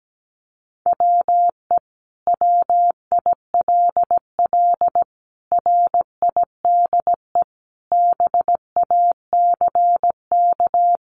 Morse code, heard as a keyed tone, about 17 wpm, 700 Hz. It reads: WE WILL RIDE BACK
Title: Sentences from Top 1200 Words 17wpm